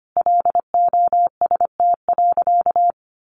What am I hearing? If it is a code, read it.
Morse code, heard as a keyed tone, about 25 wpm, 700 Hz. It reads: LOST.
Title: Sentences from Top 1400 Words 25wpm